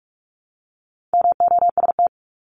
Morse code, heard as a keyed tone, about 44 wpm, 700 Hz. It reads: MOST
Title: LY Suffix - Top 100 Words with Base Words 45wpm